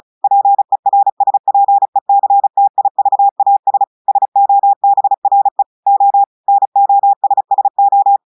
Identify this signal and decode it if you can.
Morse code, heard as a keyed tone, about 35 wpm, 800 Hz. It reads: PERSPECTIVAS SOBRE O NOSSO
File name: audios / portugais / portugais texte en clair 1 vitesse 35 mots minutes